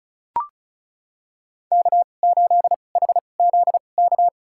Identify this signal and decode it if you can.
Morse code, heard as a keyed tone, about 36 wpm, spaced as for 25 wpm, 700 Hz. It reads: K8HZK
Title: USA Call Signs - 2x Character Spacing - 35wpm